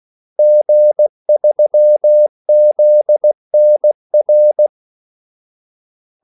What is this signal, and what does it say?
Morse code, heard as a keyed tone, about 16 wpm, 600 Hz. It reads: G3ZNR
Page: Kaldesignaler 16 wpm